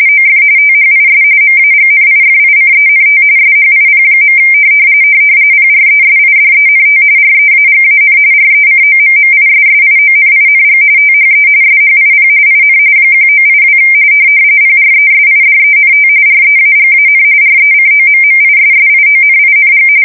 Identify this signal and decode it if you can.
RTTY, about 45.45 baud. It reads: OF THIS MAGNIFICENT SHIP. PRESIDENT REAGAN RECOMMISSIONING THE SHIP IN 1982 PRESIDENT REAGAN RECOMMISSIONING
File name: fsk170 / USS-New-Jersey